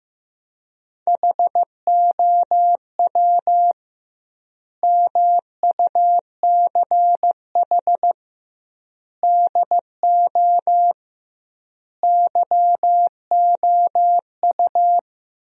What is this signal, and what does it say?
Morse code, heard as a keyed tone, about 15 wpm, 700 Hz. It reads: HOW MUCH DO YOU
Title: 2x Word-Spacing - Sentences from Top 200 Words - 15wpm